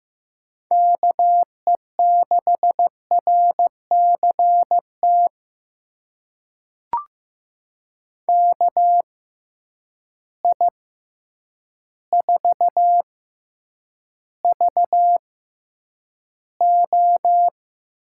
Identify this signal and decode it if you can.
Morse code, heard as a keyed tone, about 15 wpm, spaced as for 8 wpm, 700 Hz. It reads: KE6RCT KI4VO
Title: USA Call Signs - 6 Characters - 6x Character Spacing - 15wpm